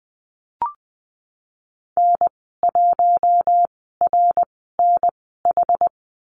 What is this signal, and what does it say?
Morse code, heard as a keyed tone, about 20 wpm, spaced as for 15 wpm, 700 Hz. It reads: N1RNH
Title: 088 Lesson 15g - Number 5 - Call Signs 20wpm